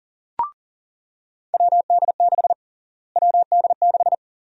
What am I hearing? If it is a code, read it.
Morse code, heard as a keyed tone, about 40 wpm, 700 Hz. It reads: WD6 WD6
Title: USA Call Sign Prefixes - Repeated 2x - 40wpm